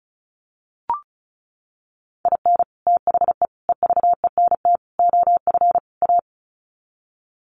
Morse code, heard as a keyed tone, about 34 wpm, 700 Hz. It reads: IN THE EVENT OF A
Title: Sets of 5 Words Encore 35wpm